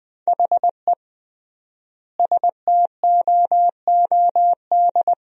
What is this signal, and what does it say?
Morse code, heard as a keyed tone, about 21 wpm, 700 Hz. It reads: HE STOOD
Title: 3x Word-Spacing - Sentences from Top 500 Words - 20wpm